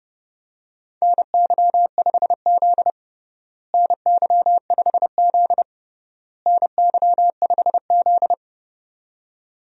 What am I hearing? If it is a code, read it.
Morse code, heard as a keyed tone, about 30 wpm, 700 Hz. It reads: NY5Z NY5Z NY5Z
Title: USA Call Signs - 4 Characters - Repeated 3x - 30wpm